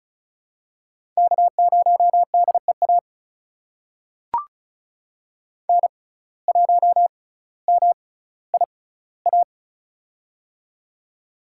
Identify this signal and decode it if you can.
Morse code, heard as a keyed tone, about 36 wpm, spaced as for 18 wpm, 700 Hz. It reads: K0DEA N1MIA
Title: USA Call Signs - 6x Character Spacing - 35wpm